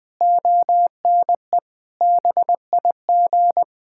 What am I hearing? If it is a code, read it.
Morse code, heard as a keyed tone, about 21 wpm, 700 Hz. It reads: ONE BIG
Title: Sentences from Top 1600 Words 20wpm